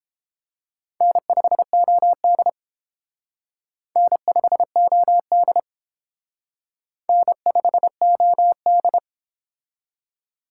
Morse code, heard as a keyed tone, about 28 wpm, 700 Hz. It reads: N5OD N5OD N5OD